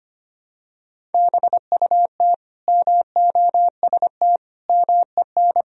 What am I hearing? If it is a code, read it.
Morse code, heard as a keyed tone, about 25 wpm, 700 Hz. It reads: BUT MOST MEN
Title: Sentences from Top 1000 Words - Review Part 1 - 25wpm